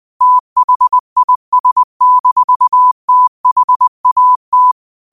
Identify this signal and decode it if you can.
Morse code, heard as a keyed tone, about 20 wpm, 1000 Hz. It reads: THIS-THAT